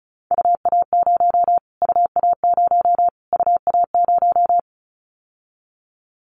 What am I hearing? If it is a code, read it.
Morse code, heard as a keyed tone, about 34 wpm, 700 Hz. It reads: UA0 UA0 UA0
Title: Callsign Prefixes for Top 25 DXCC - Repeated 3x - 35wpm